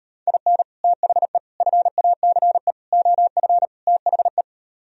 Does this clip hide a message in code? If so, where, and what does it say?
Yes: Morse code from 0.3 to 4.4 s, heard as a keyed tone, about 38 wpm, 700 Hz. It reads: IN THE FACE OF THE